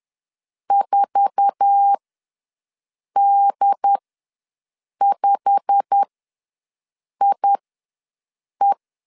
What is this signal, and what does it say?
Morse code, heard as a keyed tone, about 10 wpm, spaced as for 6 wpm, 800 Hz. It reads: 4D5IE
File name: LEZIONI / 8 YJ45 / 8 YJ45 ws 10wpm 3843